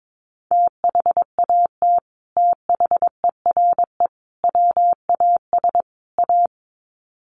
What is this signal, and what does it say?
Morse code, heard as a keyed tone, about 22 wpm, 700 Hz. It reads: THAT THERE WAS A
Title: Sets of 4 Words 22wpm